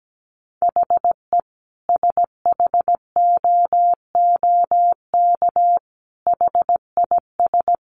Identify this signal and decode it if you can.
Morse code, heard as a keyed tone, about 17 wpm, 700 Hz. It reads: HE SHOOK HIS
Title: A Sound Of Thunder 17wpm